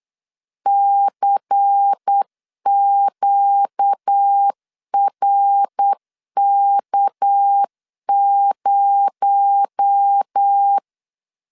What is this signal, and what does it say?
Morse code, heard as a keyed tone, about 8 wpm, 800 Hz. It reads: CQRK0
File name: LEZIONI / 6 PKX0 / 6 PKX0 8wpm 3341